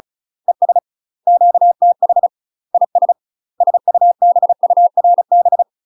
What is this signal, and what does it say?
Morse code, heard as a keyed tone, about 36 wpm, 700 Hz. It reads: ES QTH IS SUBURB
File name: UK MASTER QSOs TRAINING / 35WPM / UK MASTER QSOs 0020